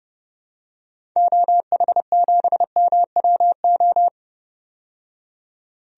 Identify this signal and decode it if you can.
Morse code, heard as a keyed tone, about 30 wpm, 700 Hz. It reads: OH7MWO
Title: DXCC-100 Call Signs - 6 Characters - 30wpm